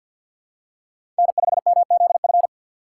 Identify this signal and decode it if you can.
Morse code, heard as a keyed tone, about 52 wpm, 700 Hz. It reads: N5MZF
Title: USA Call Signs - 5 Characters - 2x Character Spacing - 50wpm